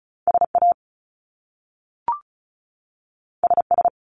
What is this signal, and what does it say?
Morse code, heard as a keyed tone, about 36 wpm, 700 Hz. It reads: SA SS
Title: Top Two Letter Combinations in Words 35wpm